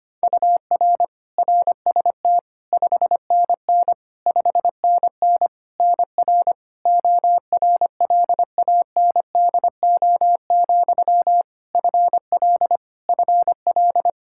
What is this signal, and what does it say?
Morse code, heard as a keyed tone, about 25 wpm, 700 Hz. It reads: UR RST 5NN 5NN NR ORLANDO, FL FL
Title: Practice QSO - Exchange 1 - Beginning - 25wpm